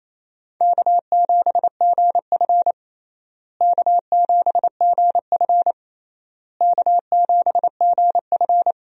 Morse code, heard as a keyed tone, about 28 wpm, 700 Hz. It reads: K7GF K7GF K7GF
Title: USA Call Signs - 4 Characters - Repeated 3x - 28wpm